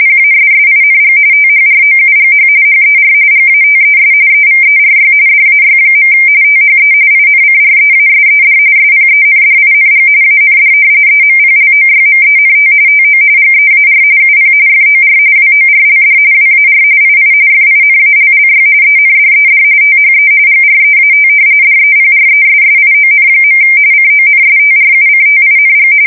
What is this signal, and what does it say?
RTTY, about 45.45 baud. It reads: PASSING THROUGH HER GATES EVERY DAY. THE INCREASED SPENDING AND EMPLOYMENT TO GEAR UP FOR WORLD WAR II FINALLY ENDED THE GREAT DEPRESSION. CONSTRUCTING
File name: fsk170 / battleshipnj